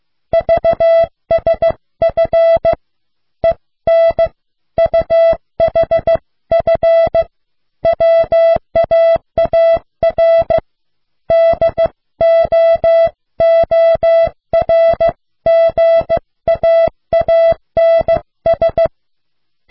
Morse code, heard as a keyed tone, about 16 wpm, 660 Hz. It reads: VSF EN UHF WAAR DOORGAANS